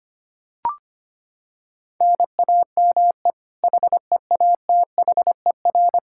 Morse code, heard as a keyed tone, about 25 wpm, 700 Hz. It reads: NAME HEATHER
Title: K1USN SST QSO Practice 25wpm